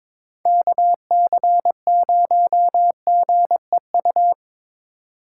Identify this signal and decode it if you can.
Morse code, heard as a keyed tone, about 26 wpm, 700 Hz. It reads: KC0GEU
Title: Speed Racing - USA Call Signs 22wpm